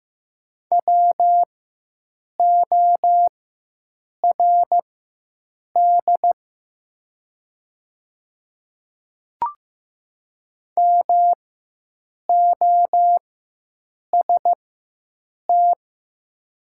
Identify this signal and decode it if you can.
Morse code, heard as a keyed tone, about 15 wpm, spaced as for 7 wpm, 700 Hz. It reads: WORD MOST
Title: 4x Character Spacing - Top 100 Words - 15wpm